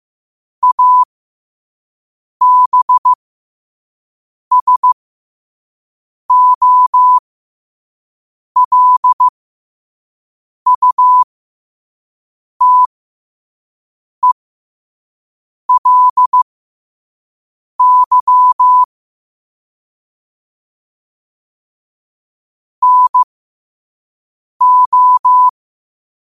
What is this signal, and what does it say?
Morse code, heard as a keyed tone, about 15 wpm, spaced as for 5 wpm, 1000 Hz. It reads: ABSOLUTELY NO
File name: qotdfiles / 5wpm